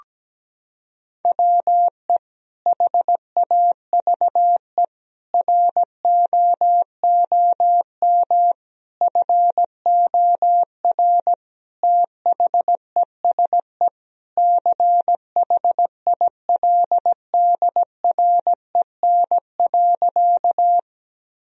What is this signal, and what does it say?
Morse code, heard as a keyed tone, about 17 wpm, 700 Hz. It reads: WE HAVE ROOM FOR THESE CHILDREN.
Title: Sentences from Top 1000 Words - Review Part 1 - 17wpm